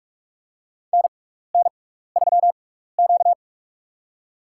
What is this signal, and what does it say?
Morse code, heard as a keyed tone, about 46 wpm, spaced as for 15 wpm, 700 Hz. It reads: NN3Q
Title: USA Call Signs - 4 Characters - 6x Character Spacing - 45wpm